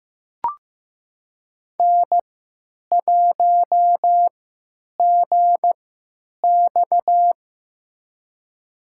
Morse code, heard as a keyed tone, about 15 wpm, spaced as for 9 wpm, 700 Hz. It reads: N1GX
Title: USA Call Signs - 3x Character Spacing - 15wpm